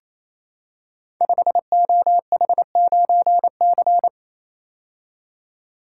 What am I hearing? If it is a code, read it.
Morse code, heard as a keyed tone, about 27 wpm, 690 Hz. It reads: 5OH9C
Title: Code Groups - 5 Letter:Numbers - 28wpm